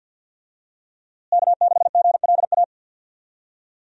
Morse code, heard as a keyed tone, about 50 wpm, 700 Hz. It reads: K6GLA
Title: USA Call Signs - 5 Characters - Repeated 3x - 50wpm